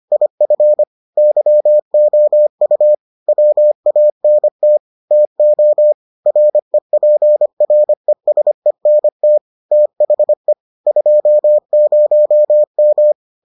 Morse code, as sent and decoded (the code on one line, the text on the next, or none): .. ..-. / -.-- --- ..- / .-- .- -. - / - --- / .-. .--. .-. ... -. - / - .... / ..--- ----- --
IF YOU WANT TO REPRESENT THE 20M